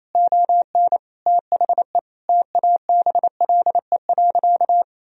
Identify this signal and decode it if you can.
Morse code, heard as a keyed tone, about 28 wpm, 700 Hz. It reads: ON THE TABLE.
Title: Sentences from Top 1000 Words - Review Part 4 - 28wpm